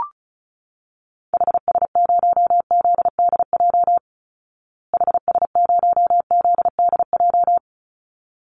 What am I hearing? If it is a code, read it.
Morse code, heard as a keyed tone, about 34 wpm, 700 Hz. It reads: HS0ZDJ HS0ZDJ
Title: Call Signs Repeated 3x - 35wpm